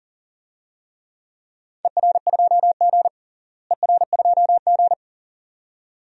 Morse code, heard as a keyed tone, about 39 wpm, 690 Hz. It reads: ER2G ER2G
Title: DXCC-100 Call Signs - 4 Characters - Repeated 2x - 40wpm